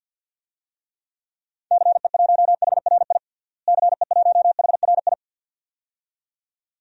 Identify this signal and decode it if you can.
Morse code, heard as a keyed tone, about 48 wpm, 700 Hz. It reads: CE1HRI CE1HRI
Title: DXCC-100 Call Signs - 6 Characters - Repeated 2x - 50wpm